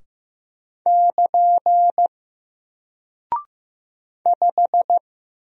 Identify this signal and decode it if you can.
Morse code, heard as a keyed tone, about 15 wpm, 700 Hz. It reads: <KN> 5
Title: Single Character (Letter:Number:Punctuation:Prosign) 15wpm